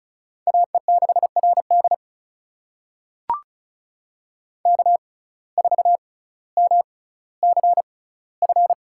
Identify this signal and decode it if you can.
Morse code, heard as a keyed tone, about 36 wpm, spaced as for 18 wpm, 700 Hz. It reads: AE6RD K4MCF
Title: USA Call Signs - 5 Characters - 6x Character Spacing - 35wpm